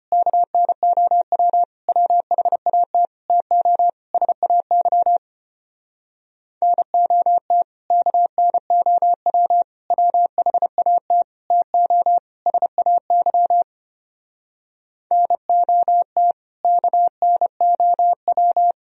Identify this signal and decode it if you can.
Morse code, heard as a keyed tone, about 33 wpm, 700 Hz. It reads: KNOW WHAT TO SAY NOT KNOW WHAT TO SAY NOT KNOW